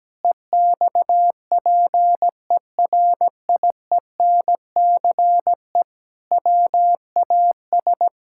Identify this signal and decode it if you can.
Morse code, heard as a keyed tone, about 17 wpm, 700 Hz. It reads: EXPERIENCE WAS
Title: Sentences from Top 1000 Words - Review Part 6 - 17wpm